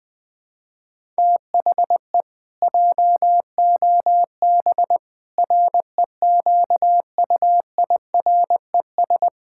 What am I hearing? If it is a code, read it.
Morse code, heard as a keyed tone, about 20 wpm, 700 Hz. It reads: THE JOB REQUIRES
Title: Pangram Challenge v2 - 20wpm